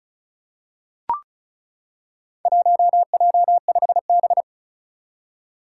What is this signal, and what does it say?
Morse code, heard as a keyed tone, about 34 wpm, 700 Hz. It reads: 1J5B
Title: Four Challenging Letter-Number Combinations 35wpm